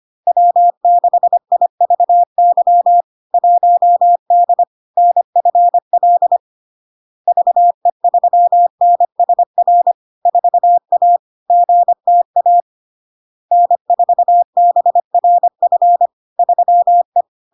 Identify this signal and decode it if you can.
Morse code, heard as a keyed tone, about 25 wpm, 700 Hz. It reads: W6IVY 1D NFL VE3NSR 4A GTA N4BRF 3E
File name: LARK-FD-2022-Log-in-CW-25wpm-700-Hz